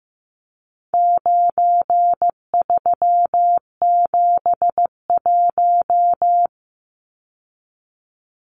Morse code, heard as a keyed tone, about 15 wpm, 700 Hz. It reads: 9371
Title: Numbers - 4-Digit 15wpm